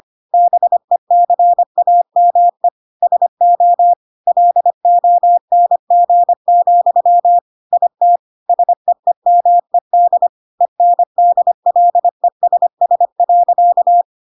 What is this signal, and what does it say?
Morse code, heard as a keyed tone, about 25 wpm, 700 Hz. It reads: BECAME SO LONG, IT SEEMED ENDLESS.
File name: Pinocchio / Pinocchio ch3 25 25 ik1hge